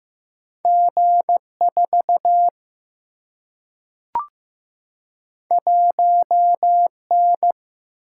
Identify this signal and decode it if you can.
Morse code, heard as a keyed tone, about 15 wpm, 700 Hz. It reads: G 4 1 N
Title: Single-Letter Single-Number 15wpm